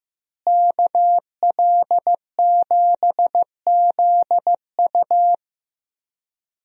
Morse code, heard as a keyed tone, about 16 wpm, 700 Hz. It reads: KL7ZU
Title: USA Call Signs - 5 Characters - Repeated 2x - 15wpm